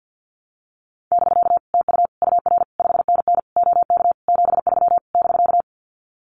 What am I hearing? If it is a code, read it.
Morse code, heard as a keyed tone, about 50 wpm, 700 Hz. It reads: <BK> TU UR 5NN OK 73 <BK>
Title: Common QSO Phrases Encore 50wpm